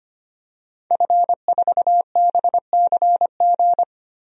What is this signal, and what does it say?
Morse code, heard as a keyed tone, about 25 wpm, 700 Hz. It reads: F4BCG
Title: DXCC-100 Call Signs - 5 Characters - 25wpm